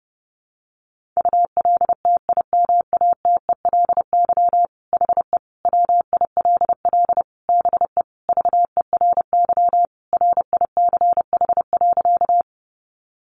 Morse code, heard as a keyed tone, about 28 wpm, 700 Hz. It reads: ULTIMATELY HE WILL BE VERY RICH.
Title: Sentences from Top 1200 Words 30wpm